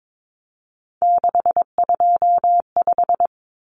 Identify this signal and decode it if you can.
Morse code, heard as a keyed tone, about 22 wpm, 700 Hz. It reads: 625
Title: Numbers - 3-Digit 22wpm